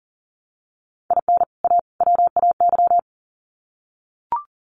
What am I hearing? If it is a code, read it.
Morse code, heard as a keyed tone, about 40 wpm, 700 Hz. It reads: IN A WAY
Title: Sets of 3 Words 40wpm